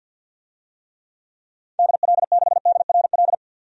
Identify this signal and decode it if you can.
Morse code, heard as a keyed tone, about 51 wpm, 700 Hz. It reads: DL6DRL
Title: DXCC-100 Call Signs - 6 Characters - 50wpm